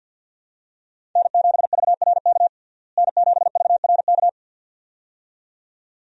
Morse code, heard as a keyed tone, about 52 wpm, 700 Hz. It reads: N7VRK N7VRK
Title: USA Call Signs - 5 Characters - Repeated 2x - 50wpm